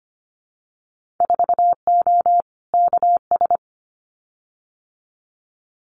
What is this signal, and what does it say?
Morse code, heard as a keyed tone, about 25 wpm, 700 Hz. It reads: 4O KS
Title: Winter Field Day Exchanges Encore 25wpm